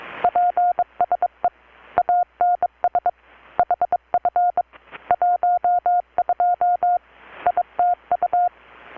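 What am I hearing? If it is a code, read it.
Morse code, heard as a keyed tone, about 23 wpm, 720 Hz. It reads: PSE ANS HF 12 ITU
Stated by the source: Morse from KPH's Night of Nights XX maritime radiotelegraphy event, received in Chiapas, Mexico about 3,800 km away on a Kenwood R-600 with a grounded 10 m long-wire antenna.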